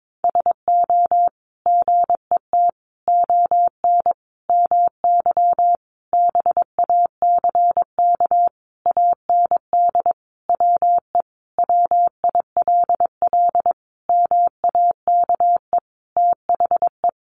Morse code, heard as a keyed tone, about 22 wpm, 700 Hz. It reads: SO GET ON MY BACK AND WE WILL MAKE THE
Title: Sentences from Top 1000 Words 22wpm